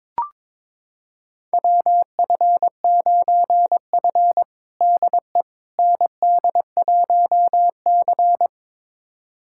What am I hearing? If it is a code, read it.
Morse code, heard as a keyed tone, about 22 wpm, 700 Hz. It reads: WF9F DE ND1C
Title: USA Callsign DE Callsign - 4 Characters - 22wpm